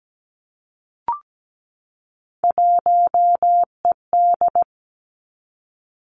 Morse code, heard as a keyed tone, about 17 wpm, 700 Hz. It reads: 1ED
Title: Three Challenging Letter-Number Combinations 17wpm